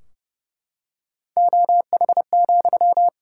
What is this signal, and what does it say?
Morse code, heard as a keyed tone, about 29 wpm, 700 Hz. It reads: OH,
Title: Sentences from Top 1600 Words 30wpm